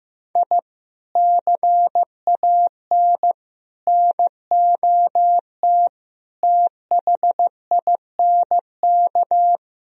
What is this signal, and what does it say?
Morse code, heard as a keyed tone, about 16 wpm, 700 Hz. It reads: I CAN NOT THINK
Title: Speed-Racing - Sets of 5 Words 15wpm